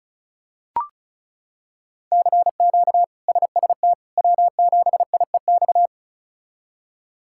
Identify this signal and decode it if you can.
Morse code, heard as a keyed tone, about 34 wpm, 700 Hz. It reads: CQ SST W7IEX
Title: K1USN SST QSO Practice - 35wpm